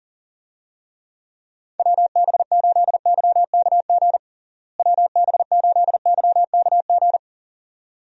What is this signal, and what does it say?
Morse code, heard as a keyed tone, about 40 wpm, 700 Hz. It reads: WB8YKG WB8YKG
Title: USA Call Signs - 6 Characters - Repeated 2x - 40wpm